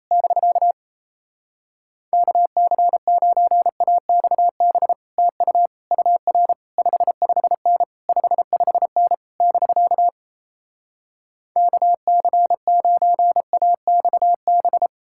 Morse code, heard as a keyed tone, about 37 wpm, 700 Hz. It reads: <BK> KC9AXB TU UR 55N 55N <BK> KC9AXB